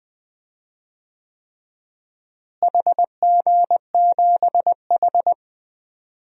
Morse code, heard as a keyed tone, about 21 wpm, 700 Hz. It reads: HG7H
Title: DXCC-100 Call Signs - 4 Characters - Repeated 3x - 20wpm